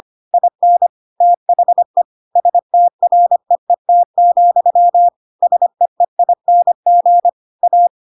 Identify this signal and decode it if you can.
Morse code, heard as a keyed tone, about 25 wpm, 700 Hz. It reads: IN THE STREET, SEEING A
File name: Pinocchio / Pinocchio ch3 25 25 ik1hge